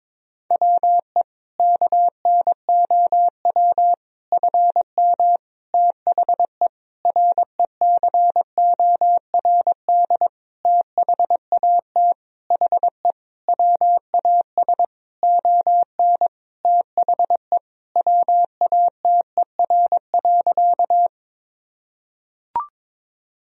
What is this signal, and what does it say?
Morse code, heard as a keyed tone, about 22 wpm, 700 Hz. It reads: WE KNOW FM THE RECORD THAT HE WAS ON THE WATER.
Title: Sentences from Top 500 Words and Abbreviations 22wpm